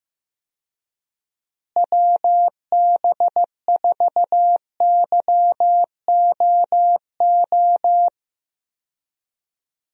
Morse code, heard as a keyed tone, about 15 wpm, 700 Hz. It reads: WB4YOO
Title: USA Call Signs - 6 Characters - Repeated 3x - 15wpm